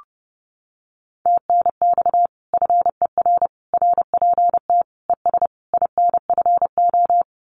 Morse code, heard as a keyed tone, about 28 wpm, 700 Hz. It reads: TNX FER RPT ES INFO
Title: Common QSO Phrases Encore 30wpm